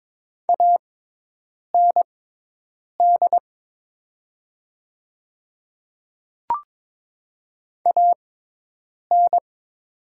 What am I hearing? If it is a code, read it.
Morse code, heard as a keyed tone, about 22 wpm, spaced as for 8 wpm, 690 Hz. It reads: AND AN
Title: 6x Character Spacing - Top 100 Words - 22wpm